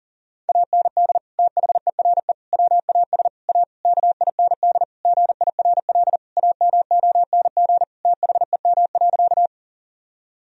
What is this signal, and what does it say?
Morse code, heard as a keyed tone, about 40 wpm, 700 Hz. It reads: AND THERE WAS A KIND GIRL AMONG THEM.
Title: Sentences from Top 1000 Words - Review Part 3 - 40wpm